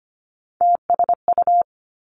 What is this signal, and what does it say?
Morse code, heard as a keyed tone, about 25 wpm, 700 Hz. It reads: TSU
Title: Three Challenging Letter-Number Combinations 25wpm